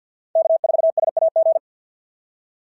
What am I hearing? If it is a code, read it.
Morse code, heard as a keyed tone, about 52 wpm, 690 Hz. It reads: K4SAG
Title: SOTA QSO Practice 50wpm